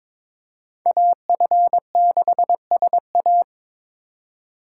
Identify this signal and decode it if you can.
Morse code, heard as a keyed tone, about 26 wpm, 700 Hz. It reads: AF6SA
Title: Speed Racing - USA 2x2 Call Signs 22wpm